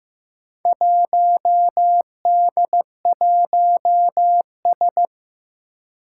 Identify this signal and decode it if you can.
Morse code, heard as a keyed tone, about 15 wpm, 700 Hz. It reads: 1D1S
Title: Four Challenging Letter-Number Combinations 15wpm